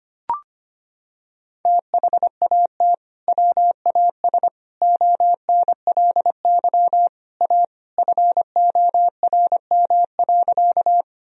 Morse code, heard as a keyed tone, about 25 wpm, 700 Hz. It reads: THAT WAS ONLY A FORM.
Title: Sentences from Top 1000 Words - Review Part 1 - 25wpm